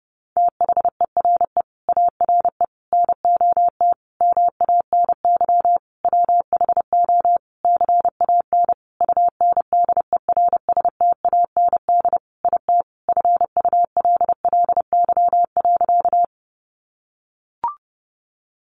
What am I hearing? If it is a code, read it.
Morse code, heard as a keyed tone, about 30 wpm, 700 Hz. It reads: THERE ARE NOT MANY WHO CAN UNDERSTAND IT FULLY.
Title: Sentences from Top 1300 Words 30wpm